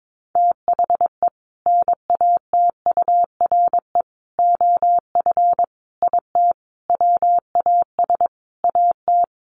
Morse code, heard as a keyed tone, about 22 wpm, 700 Hz. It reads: THE NATURE OF IT WAS AT
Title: Sentences from Top 1400 Words 22wpm